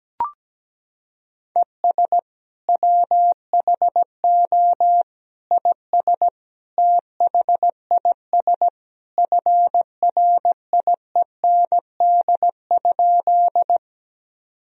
Morse code, heard as a keyed tone, about 17 wpm, 700 Hz. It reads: ES WHO IS THIS FRIEND?
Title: Sentences from Top 300 Words and Abbreviations 17wpm